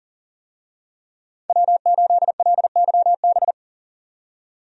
Morse code, heard as a keyed tone, about 40 wpm, 700 Hz. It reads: W8LYB
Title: USA Call Signs - 5 Characters - 40wpm